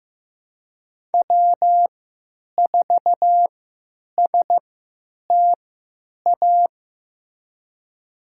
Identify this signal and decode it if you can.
Morse code, heard as a keyed tone, about 15 wpm, spaced as for 9 wpm, 700 Hz. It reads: W4STA
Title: USA Call Signs - 5 Characters - 3x Character Spacing - 15wpm